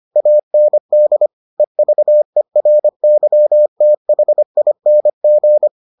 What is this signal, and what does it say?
Morse code, heard as a keyed tone, about 25 wpm, 600 Hz. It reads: AND EVERYTHING